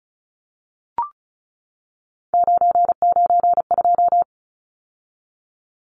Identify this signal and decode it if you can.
Morse code, heard as a keyed tone, about 34 wpm, 700 Hz. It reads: 992
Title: Numbers - 3-Digit 35wpm